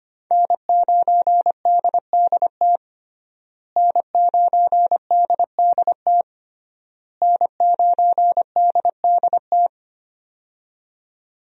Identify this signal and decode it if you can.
Morse code, heard as a keyed tone, about 25 wpm, 700 Hz. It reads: N9DDT N9DDT N9DDT
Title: USA Call Signs - 5 Characters - Repeated 3x - 25wpm